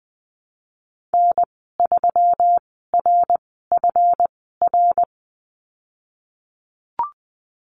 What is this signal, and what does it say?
Morse code, heard as a keyed tone, about 20 wpm, spaced as for 15 wpm, 700 Hz. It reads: N3RFR
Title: 140 Lesson 23g - Letter f - Call Signs 20wpm